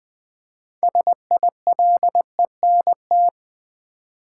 Morse code, heard as a keyed tone, about 20 wpm, 700 Hz. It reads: SILENT
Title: Build A Word - 2 Letters At A Time - 6-Letter Words - 20wpm